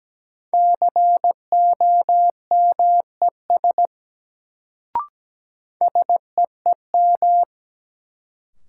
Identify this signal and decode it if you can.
Morse code, heard as a keyed tone, about 17 wpm, 700 Hz. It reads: COMES SEEM
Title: Sets of 3 Words - New Word Drill - 17wpm